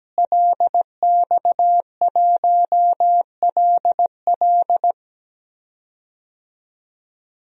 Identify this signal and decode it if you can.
Morse code, heard as a keyed tone, about 17 wpm, 700 Hz. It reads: LX1LL
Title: DXCC-100 Call Signs - 5 Characters - Repeated 2x - 17wpm